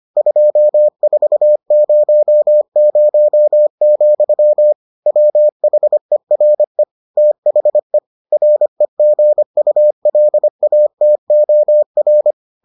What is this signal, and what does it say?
Morse code, heard as a keyed tone, about 25 wpm, 600 Hz. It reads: 2400, WHERE THE REGULATOR